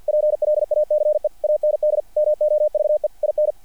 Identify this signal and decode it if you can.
Morse code, heard as a keyed tone, about 50 wpm, 600 Hz. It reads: PLACE AND MOVE IN
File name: archived / CRSnet